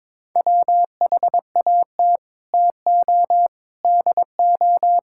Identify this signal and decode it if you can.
Morse code, heard as a keyed tone, about 22 wpm, 700 Hz. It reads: WHAT TO DO
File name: practice / 22wpm / Sets of 8 Words - 22wpm